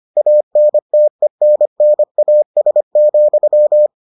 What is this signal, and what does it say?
Morse code, heard as a keyed tone, about 25 wpm, 600 Hz. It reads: ANTENNAS,